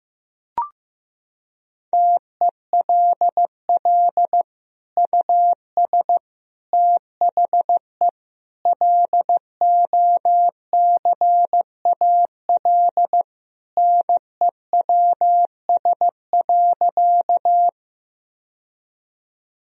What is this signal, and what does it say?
Morse code, heard as a keyed tone, about 16 wpm, 700 Hz. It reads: TELL US THE LOCAL NEWS.
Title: Sentences from Top 1000 Words - Review Part 3 - 15wpm